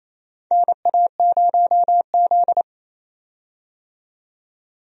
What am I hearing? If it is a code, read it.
Morse code, heard as a keyed tone, about 28 wpm, 700 Hz. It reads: NA0Z
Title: USA Call Signs - 4 Characters - 28wpm